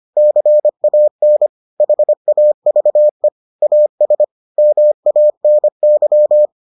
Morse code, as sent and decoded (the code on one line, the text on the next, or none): -.-. .- -. / .... .- ...- . / .- ... / -- .- -. -.--
CAN HAVE AS MANY